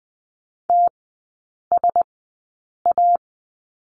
Morse code, heard as a keyed tone, about 20 wpm, spaced as for 8 wpm, 700 Hz. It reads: TSA
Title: 047 Lesson 9e - Number 4 - Three Characters 20wpm